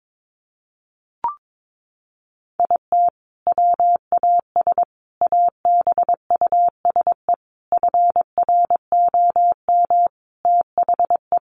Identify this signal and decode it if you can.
Morse code, heard as a keyed tone, about 22 wpm, 700 Hz. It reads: IT WAS ABUSE FROM THE
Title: Sentences from Top 1300 Words 22wpm